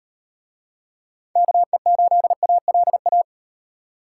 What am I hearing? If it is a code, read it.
Morse code, heard as a keyed tone, about 38 wpm, 700 Hz. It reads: KE8ALA